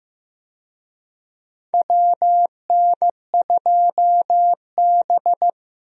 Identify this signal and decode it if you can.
Morse code, heard as a keyed tone, about 15 wpm, 700 Hz. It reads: WN2B
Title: USA Call Signs - 4 Characters - Repeated 3x - 15wpm